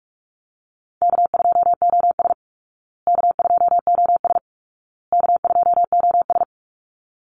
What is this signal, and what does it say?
Morse code, heard as a keyed tone, about 44 wpm, 700 Hz. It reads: K2OS K2OS K2OS
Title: Call Signs Repeated 2x - 45wpm